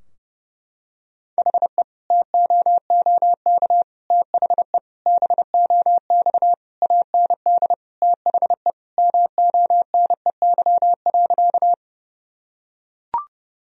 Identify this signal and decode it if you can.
Morse code, heard as a keyed tone, about 32 wpm, 700 Hz. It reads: HE TOOK THE BOX AND THE MONEY.
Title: Sentences from Top 1000 Words - Review Part 2 - 30wpm